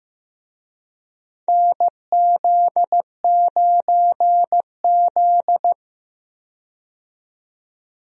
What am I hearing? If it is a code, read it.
Morse code, heard as a keyed tone, about 15 wpm, 700 Hz. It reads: NZ9Z
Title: USA Call Signs - 4 Characters - 15wpm